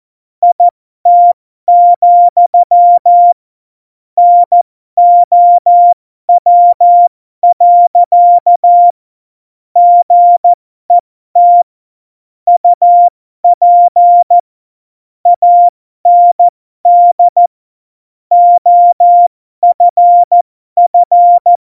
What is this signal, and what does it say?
Morse code, heard as a keyed tone, about 14 wpm, spaced as for 12 wpm, 700 Hz. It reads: IT, NOW. GET UP AND OFF
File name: Morsekurs / Witze